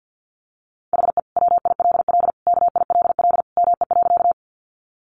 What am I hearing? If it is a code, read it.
Morse code, heard as a keyed tone, about 48 wpm, 700 Hz. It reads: HE WILL KILL ME.
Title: Sentences from Top 1400 Words 50wpm